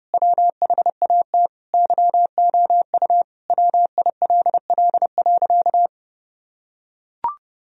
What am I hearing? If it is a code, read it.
Morse code, heard as a keyed tone, about 32 wpm, 700 Hz. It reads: WHAT YOU WILL.
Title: Sentences from Top 1000 Words - Review Part 2 - 30wpm